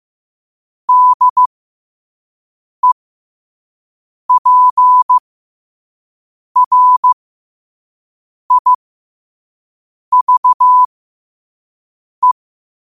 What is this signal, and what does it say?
Morse code, heard as a keyed tone, about 15 wpm, spaced as for 5 wpm, 1000 Hz. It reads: DEPRIVE